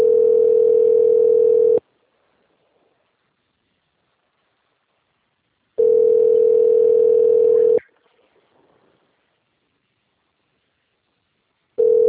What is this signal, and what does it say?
Ringback tone (North America), 440 + 480 Hz.